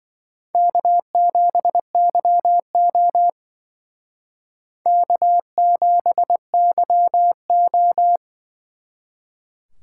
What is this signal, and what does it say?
Morse code, heard as a keyed tone, about 23 wpm, 700 Hz. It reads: K7YO K7YO
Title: Speed Racing - USA 1x2 Call Signs 20wpm